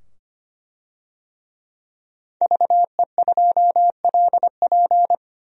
Morse code, heard as a keyed tone, about 25 wpm, 700 Hz. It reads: VE2LP
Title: DXCC-100 Call Signs - 5 Characters - 25wpm